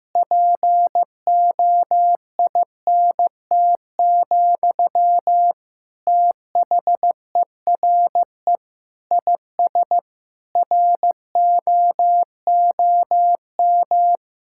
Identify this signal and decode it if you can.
Morse code, heard as a keyed tone, about 15 wpm, 700 Hz. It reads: POINT, THERE IS ROOM
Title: Sentences from Top 1600 Words 15wpm